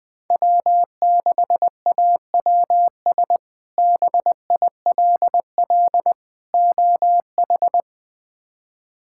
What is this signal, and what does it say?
Morse code, heard as a keyed tone, about 21 wpm, 700 Hz. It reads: W6AWS BILL OH
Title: K1USN SST QSO Practice - 20wpm